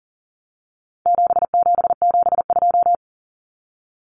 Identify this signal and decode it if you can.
Morse code, heard as a keyed tone, about 40 wpm, 700 Hz. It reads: 7772
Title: Numbers - 4-Digit 40wpm